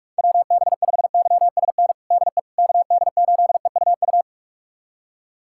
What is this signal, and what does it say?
Morse code, heard as a keyed tone, about 46 wpm, 700 Hz. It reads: WB5YSN DE KD8EUU
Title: USA Callsign DE Callsign - 6 Characters - 45wpm